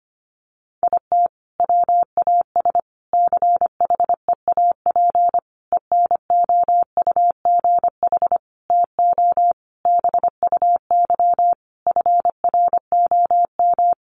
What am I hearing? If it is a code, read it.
Morse code, heard as a keyed tone, about 25 wpm, 700 Hz. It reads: IT WAS CHEAP ENOUGH TO BUY FROM
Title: Sentences from Top 1200 Words 25wpm